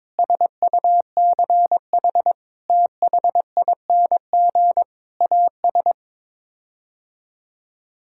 Morse code, heard as a keyed tone, about 22 wpm, 700 Hz. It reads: SUCH THING AS
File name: practice / 22wpm / Sets of 6 Words - 22wpm